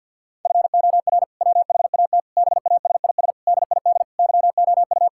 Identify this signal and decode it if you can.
Morse code, heard as a keyed tone, about 50 wpm, 700 Hz. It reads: FOR WHAT BASIS DID YOU
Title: Sentences from Top 1000 Words - Review Part 5 - 50wpm